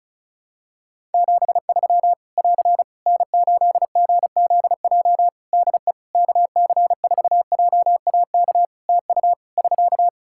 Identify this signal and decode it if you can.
Morse code, heard as a keyed tone, about 36 wpm, 700 Hz. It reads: 73 <AR> N8GZJ DE KC4JAK TU <SK>
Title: Practice QSO - Exchange 4 - Ending - 35wpm